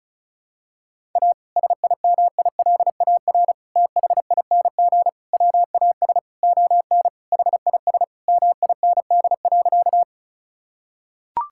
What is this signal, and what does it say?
Morse code, heard as a keyed tone, about 36 wpm, 700 Hz. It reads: A SIMILAR THING WAS ON HIS MIND.
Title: Sentences from Top 1000 Words - Review Part 5 - 35wpm